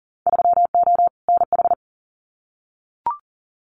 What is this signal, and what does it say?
Morse code, heard as a keyed tone, about 40 wpm, 700 Hz. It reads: 3O NH
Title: Winter Field Day Exchanges Encore 40wpm